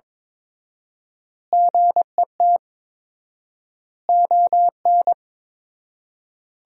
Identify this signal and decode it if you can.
Morse code, heard as a keyed tone, about 21 wpm, 700 Hz. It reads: GET ON
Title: 4x Word-Spacing - Sentences from Top 200 Words - 22wpm